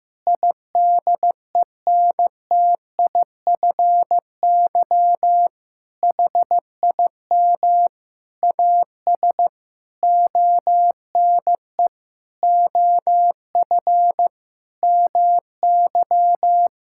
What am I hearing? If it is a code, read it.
Morse code, heard as a keyed tone, about 15 wpm, 700 Hz. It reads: IDENTIFY HIM AS ONE OF MY
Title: Sentences from Top 1000 Words - Review Part 6 - 15wpm